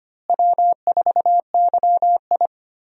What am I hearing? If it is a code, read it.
Morse code, heard as a keyed tone, about 25 wpm, 700 Hz. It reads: W4YI
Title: USA Call Signs - 4 Characters - Repeated 2x - 25wpm